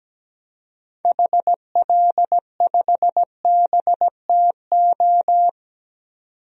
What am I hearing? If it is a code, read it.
Morse code, heard as a keyed tone, about 17 wpm, 700 Hz. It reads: HL5BTO
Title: DXCC-100 Call Signs - 6 Characters - 17wpm